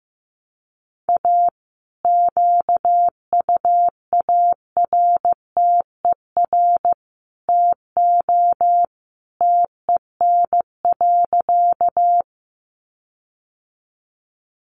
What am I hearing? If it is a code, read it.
Morse code, heard as a keyed tone, about 15 wpm, 700 Hz. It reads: A QUARTER TO TEN.
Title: Sentences from Top 1300 Words 15wpm